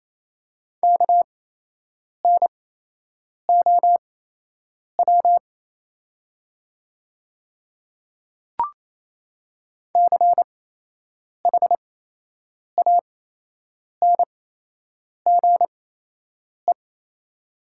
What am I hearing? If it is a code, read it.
Morse code, heard as a keyed tone, about 28 wpm, spaced as for 8 wpm, 700 Hz. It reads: KNOW CHANGE
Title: 8x Character Spacing - Top 200 Words - 28wpm